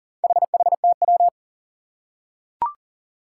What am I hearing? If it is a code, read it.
Morse code, heard as a keyed tone, about 43 wpm, 700 Hz. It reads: HHTW
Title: Four Challenging Letter-Number Combinations 40wpm